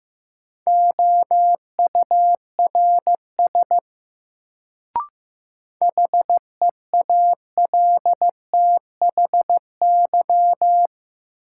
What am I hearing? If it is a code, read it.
Morse code, heard as a keyed tone, about 15 wpm, 700 Hz. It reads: OURS HEALTHY
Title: Sentences from Top 1000 Words - Review Part 5 - New Word Drill - 15wpm